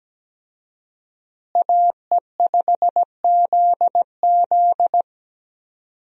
Morse code, heard as a keyed tone, about 17 wpm, 700 Hz. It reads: AE5ZZ
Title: USA Call Signs - 5 Characters - Repeated 2x - 17wpm